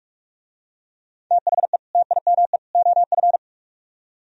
Morse code, heard as a keyed tone, about 46 wpm, 700 Hz. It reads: THE TIME OF